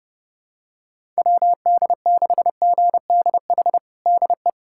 Morse code, heard as a keyed tone, about 30 wpm, 700 Hz. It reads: WD6GDH DE
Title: USA Callsign DE Callsign - 6 Characters - 30wpm